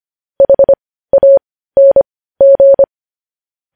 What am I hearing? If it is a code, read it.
Morse code, heard as a keyed tone, about 25 wpm, spaced as for 15 wpm, 550 Hz. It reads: HANG
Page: news-headlines-in-morse-code-at-15-wpm